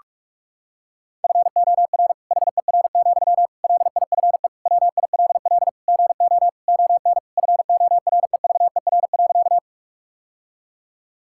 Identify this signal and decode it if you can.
Morse code, heard as a keyed tone, about 41 wpm, 700 Hz. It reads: FOR HER, LIFE WILL GO ON FOREVER.
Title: Sentences from Top 1600 Words 45wpm